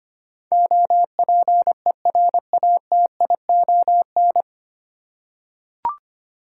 Morse code, heard as a keyed tone, about 25 wpm, 700 Hz. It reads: OPERATION
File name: practice / 25wpm / 9 Letter Words - First 3 Letters Given - 25wpm